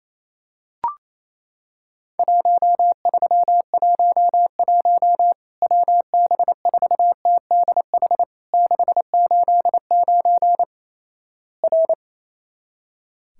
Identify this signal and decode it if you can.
Morse code, heard as a keyed tone, about 28 wpm, 700 Hz. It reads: 1311 WB4TDH 689 R
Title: Worked All Europe DX Contest - QTC Exchange - 28wpm